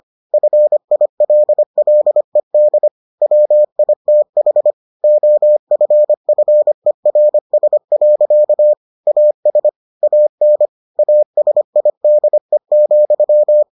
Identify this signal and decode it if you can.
Morse code, heard as a keyed tone, about 25 wpm, 600 Hz. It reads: FILLED WITH OFFERS. AS AN ASIDE,